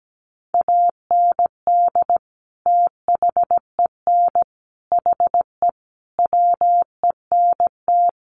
Morse code, heard as a keyed tone, about 17 wpm, 700 Hz. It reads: AND THEN HE WENT
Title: Sentences from Top 1100 Words 17wpm